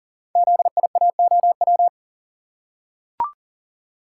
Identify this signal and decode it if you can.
Morse code, heard as a keyed tone, about 38 wpm, 700 Hz. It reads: ZIAOW
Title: Code Groups - 5 Letter:Numbers - 40wpm